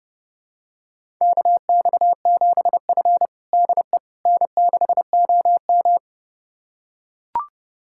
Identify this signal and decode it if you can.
Morse code, heard as a keyed tone, about 30 wpm, 700 Hz. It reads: KX7F DE N6OM
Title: USA Callsign DE Callsign - 4 Characters - 30wpm